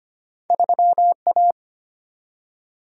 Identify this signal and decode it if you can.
Morse code, heard as a keyed tone, about 25 wpm, 700 Hz. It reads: 3 A
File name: Two Letter-Number Combinations - 25wpm